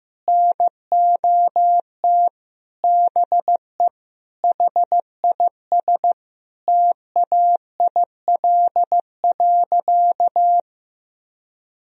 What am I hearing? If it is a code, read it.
Morse code, heard as a keyed tone, about 15 wpm, 700 Hz. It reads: NOT BE HIS TAIL.
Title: Sentences from Top 1000 Words - Review Part 3 - 15wpm